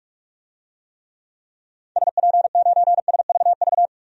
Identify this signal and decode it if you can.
Morse code, heard as a keyed tone, about 46 wpm, 700 Hz. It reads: SP9SVV
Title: DXCC-100 Call Signs - 6 Characters - 45wpm